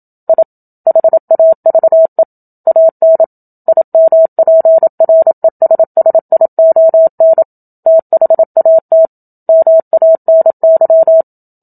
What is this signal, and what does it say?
Morse code, heard as a keyed tone, about 27 wpm, 650 Hz. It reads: I HAVE AN IMPRESSION THAT MANY
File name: Book Two / 01-Chapter 1 - 27 WPM